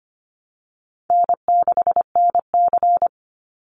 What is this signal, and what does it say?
Morse code, heard as a keyed tone, about 25 wpm, 700 Hz. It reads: N6NC
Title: Easy Call Signs - 25wpm